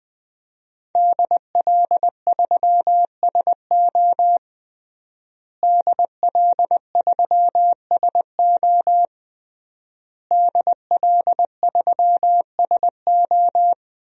Morse code, heard as a keyed tone, about 20 wpm, 700 Hz. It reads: DL3SO DL3SO DL3SO
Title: DXCC-100 Call Signs - 5 Characters - Repeated 3x - 20wpm